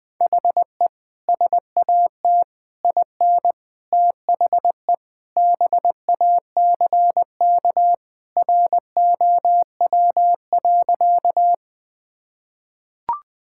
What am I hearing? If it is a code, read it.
Morse code, heard as a keyed tone, about 20 wpm, 700 Hz. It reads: HE SAT IN THE BACK ROW.
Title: Sentences with 3-Word Phrases - 20wpm